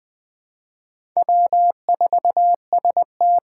Morse code, heard as a keyed tone, about 20 wpm, 700 Hz. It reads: W4ST
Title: Four Challenging Letter-Number Combinations 20wpm